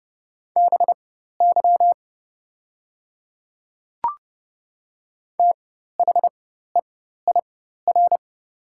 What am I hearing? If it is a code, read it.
Morse code, heard as a keyed tone, about 30 wpm, spaced as for 14 wpm, 700 Hz. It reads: BY THEIR
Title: 4x Character Spacing - Top 100 Words - 30wpm